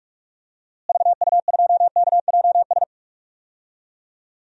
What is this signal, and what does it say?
Morse code, heard as a keyed tone, about 42 wpm, 700 Hz. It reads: VU2KJS
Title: DXCC-100 Call Signs - 6 Characters - Repeated 3x - 45wpm